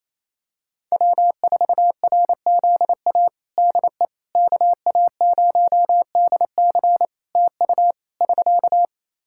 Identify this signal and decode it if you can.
Morse code, heard as a keyed tone, about 28 wpm, 700 Hz. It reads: W4RZA DE KA0DC TU <SK>
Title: Practice QSO - Exchange 4 - Ending - 28wpm